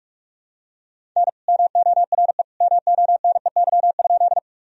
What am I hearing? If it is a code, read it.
Morse code, heard as a keyed tone, about 46 wpm, 700 Hz. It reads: N MORE MONEY?
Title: Sentences from Top 400 Words and Abbreviations 45wpm